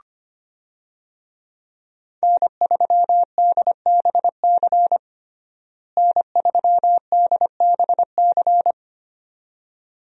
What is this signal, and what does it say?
Morse code, heard as a keyed tone, about 25 wpm, 700 Hz. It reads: N3DBC N3DBC
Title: USA Call Signs - 5 Characters - Repeated 2x - 25wpm